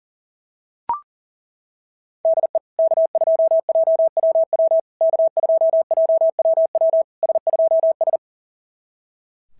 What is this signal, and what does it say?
Morse code, heard as a keyed tone, about 40 wpm, 690 Hz. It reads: DE K2JWW K2JWW S2S
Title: SOTA QSO Practice 40wpm